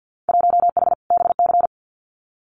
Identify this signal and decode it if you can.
Morse code, heard as a keyed tone, about 50 wpm, 700 Hz. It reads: 1H BC
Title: Winter Field Day Exchanges 50wpm